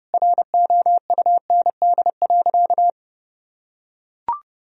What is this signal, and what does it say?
Morse code, heard as a keyed tone, about 30 wpm, 700 Hz. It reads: ROUND.
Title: Sentences from Top 1000 Words - Review Part 1 - 30wpm